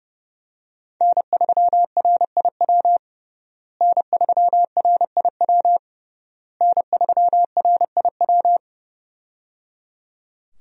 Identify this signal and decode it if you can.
Morse code, heard as a keyed tone, about 30 wpm, 700 Hz. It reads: N3RIW N3RIW N3RIW
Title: USA Call Signs - 5 Characters - Repeated 3x - 30wpm